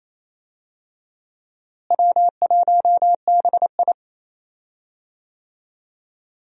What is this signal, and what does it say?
Morse code, heard as a keyed tone, about 28 wpm, 700 Hz. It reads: W1BI
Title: USA Call Signs - 4 Characters - Repeated 3x - 28wpm